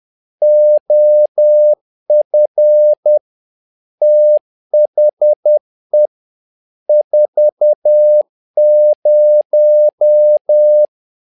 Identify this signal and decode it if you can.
Morse code, heard as a keyed tone, about 10 wpm, 600 Hz. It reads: OF THE 40
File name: learning center / tower trouble / tower trouble 10wpm